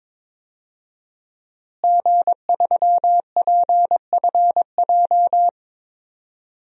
Morse code, heard as a keyed tone, about 22 wpm, 700 Hz. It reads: G3PFJ
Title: DXCC-100 Call Signs - 5 Characters - Repeated 2x - 22wpm